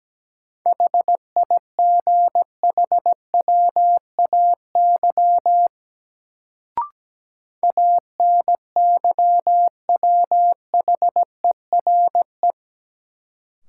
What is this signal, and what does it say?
Morse code, heard as a keyed tone, about 17 wpm, 700 Hz. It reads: HIGHWAY ANYWHERE
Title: Sentences from Top 1400 Words - New Word Drill - 17wpm